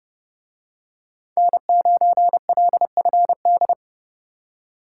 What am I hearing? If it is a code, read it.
Morse code, heard as a keyed tone, about 29 wpm, 700 Hz. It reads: N9LFD